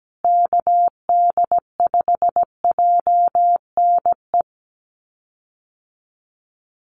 Morse code, heard as a keyed tone, about 17 wpm, 700 Hz. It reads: KD5JNE
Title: USA Call Signs Repeated 3x 17wpm